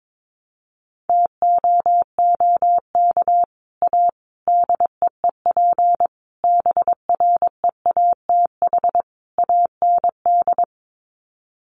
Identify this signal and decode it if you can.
Morse code, heard as a keyed tone, about 22 wpm, 700 Hz. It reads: TOOK A DEEP BREATH AND
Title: Sets of 5 Words Encore 22wpm